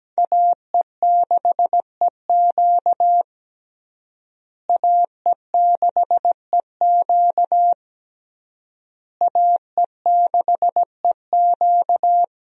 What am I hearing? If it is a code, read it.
Morse code, heard as a keyed tone, about 17 wpm, 700 Hz. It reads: AE6EQ AE6EQ AE6EQ
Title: USA Call Signs - 5 Characters - Repeated 3x - 17wpm